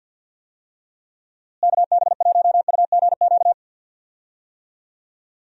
Morse code, heard as a keyed tone, about 48 wpm, 700 Hz. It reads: KB1UGQ
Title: USA Call Signs - 6 Characters - Repeated 3x - 50wpm